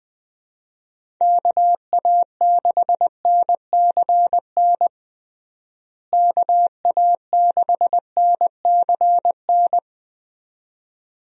Morse code, heard as a keyed tone, about 20 wpm, 700 Hz. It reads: KA6NCN KA6NCN
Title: USA Call Signs - 6 Characters - Repeated 2x - 20wpm